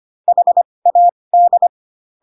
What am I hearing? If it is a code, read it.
Morse code, heard as a keyed tone, about 25 wpm, spaced as for 20 wpm, 700 Hz. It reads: HAD